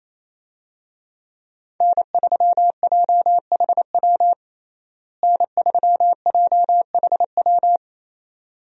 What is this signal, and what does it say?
Morse code, heard as a keyed tone, about 28 wpm, 700 Hz. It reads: N3JHW N3JHW
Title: USA Call Signs - 5 Characters - Repeated 2x - 28wpm